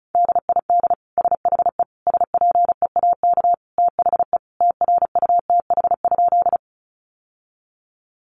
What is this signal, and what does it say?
Morse code, heard as a keyed tone, about 34 wpm, 700 Hz. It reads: DID SHE SPEAK THE TRUTH?
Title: Sentences from Top 1100 Words 35wpm